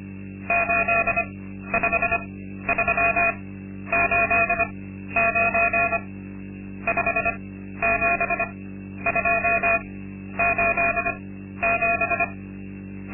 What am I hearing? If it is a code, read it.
Morse code, heard as a keyed tone, about 25 wpm, spaced as for 12 wpm, 680 Hz. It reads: 85389 57287
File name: Linksmakalnis-2